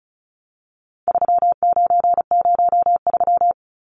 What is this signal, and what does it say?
Morse code, heard as a keyed tone, about 34 wpm, 700 Hz. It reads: 3903
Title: Numbers - 4-Digit 35wpm